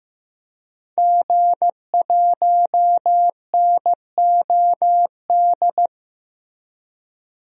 Morse code, heard as a keyed tone, about 15 wpm, 700 Hz. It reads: G1NOD
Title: DXCC-100 Call Signs - 5 Characters - Repeated 2x - 15wpm